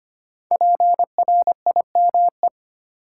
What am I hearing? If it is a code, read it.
Morse code, heard as a keyed tone, about 25 wpm, 700 Hz. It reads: PRIME
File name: 5 Letter Words - First 3 Letters Given - 25wpm